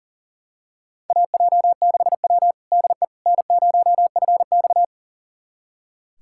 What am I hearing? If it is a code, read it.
Morse code, heard as a keyed tone, about 40 wpm, 700 Hz. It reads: AJ6W DE N0FX
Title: USA Callsign DE Callsign - 4 Characters - 40wpm